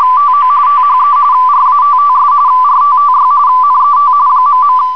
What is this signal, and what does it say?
RTTY, about 45.45 baud. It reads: TO THE RATTS RTTY NET ..